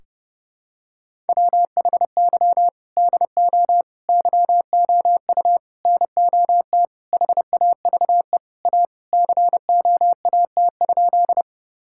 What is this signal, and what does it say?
Morse code, heard as a keyed tone, about 30 wpm, 700 Hz. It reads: WHY DO YOU NOT HAVE A COAT?
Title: Sentences from Top 1600 Words 30wpm